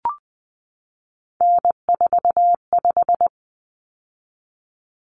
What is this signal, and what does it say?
Morse code, heard as a keyed tone, about 20 wpm, 700 Hz. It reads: N45
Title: Common Cut Numbers - 3-Digit 20wpm